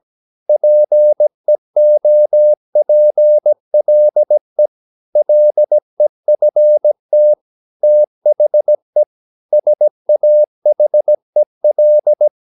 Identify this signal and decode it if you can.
Morse code, heard as a keyed tone, about 17 wpm, 600 Hz. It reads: PEOPLE LEFT THE SAHEL
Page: Romanen Never af Ken Follet på engelsk.